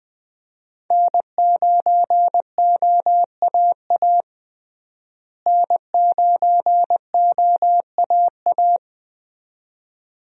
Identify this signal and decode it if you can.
Morse code, heard as a keyed tone, about 20 wpm, 700 Hz. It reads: N9OAA N9OAA
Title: USA Call Signs - 5 Characters - Repeated 2x - 20wpm